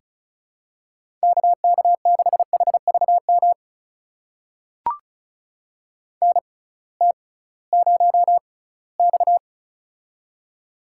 Morse code, heard as a keyed tone, about 36 wpm, spaced as for 20 wpm, 700 Hz. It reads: KK6HVM NT0X
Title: USA Call Signs - 6x Character Spacing - 35wpm